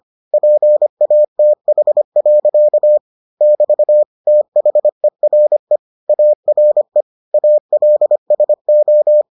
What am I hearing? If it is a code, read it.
Morse code, heard as a keyed tone, about 25 wpm, 600 Hz. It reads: PATH. <BT> THERE ARE ALSO